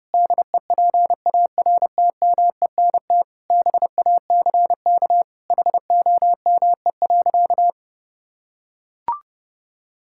Morse code, heard as a keyed tone, about 32 wpm, 700 Hz. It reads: DEPARTMENT BACK HOME.
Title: Sentences from Top 1000 Words - Review Part 6 - 30wpm